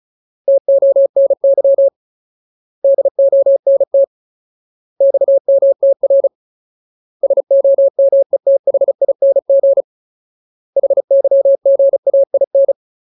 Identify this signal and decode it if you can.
Morse code, heard as a keyed tone, about 34 wpm, 550 Hz. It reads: TONY DONT XMTR SOMETHING HYGAIN
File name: Ultimate Word List / 35WPM Ultimate